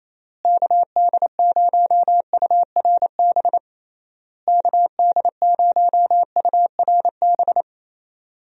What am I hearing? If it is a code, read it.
Morse code, heard as a keyed tone, about 28 wpm, 700 Hz. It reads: KD0URB KD0URB
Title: USA Call Signs - 6 Characters - Repeated 3x - 28wpm